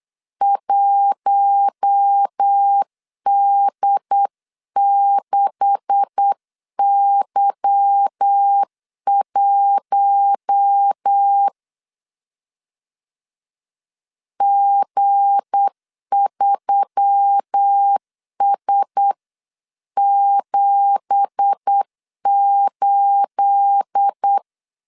Morse code, heard as a keyed tone, about 8 wpm, 800 Hz. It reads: 1D6Y1 G3S78
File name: CW / DET / 1 9 1500C S C 8wpm 5653